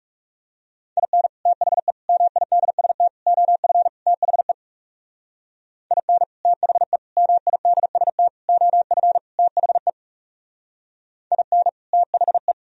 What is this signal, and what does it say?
Morse code, heard as a keyed tone, about 35 wpm, 700 Hz. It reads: IN THE MIDST OF THE IN THE MIDST OF THE IN THE
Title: Speed-Racing - Sets of 5 Words 30wpm